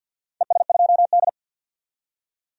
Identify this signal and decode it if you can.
Morse code, heard as a keyed tone, about 47 wpm, 700 Hz. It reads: ES2D
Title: DXCC-100 Call Signs - 4 Characters - 50wpm